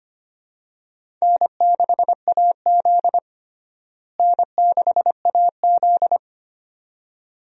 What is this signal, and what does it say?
Morse code, heard as a keyed tone, about 25 wpm, 700 Hz. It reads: N6AZ N6AZ
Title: USA Call Signs - 4 Characters - Repeated 2x - 25wpm